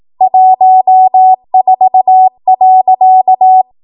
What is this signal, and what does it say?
Morse code, heard as a keyed tone, about 18 wpm, 750 Hz. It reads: 14.
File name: Archive / 18 WPM / 18WPM